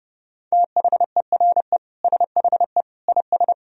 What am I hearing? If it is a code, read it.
Morse code, heard as a keyed tone, about 30 wpm, 700 Hz. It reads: THERE SHE IS
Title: Sentences from Top 200 Words and Abbreviations 30wpm